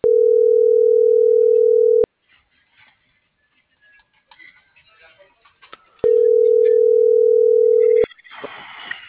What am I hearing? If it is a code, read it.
Ringback tone (North America), 440 + 480 Hz.